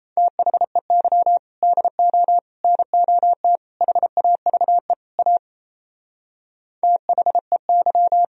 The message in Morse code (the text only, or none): THEY DO NOT HAVE A THEY